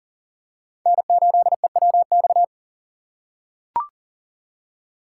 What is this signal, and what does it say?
Morse code, heard as a keyed tone, about 37 wpm, 700 Hz. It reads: N8EWX
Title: K1USN SST QSO Practice - 40wpm